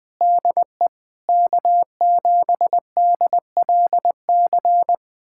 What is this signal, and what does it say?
Morse code, heard as a keyed tone, about 20 wpm, 700 Hz. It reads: DE K7DLC
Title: USA Callsign DE Callsign - 5 Characters - 20wpm